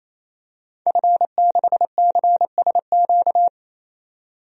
Morse code, heard as a keyed tone, about 27 wpm, 700 Hz. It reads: F6CSQ
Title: DXCC-100 Call Signs - 5 Characters - 28wpm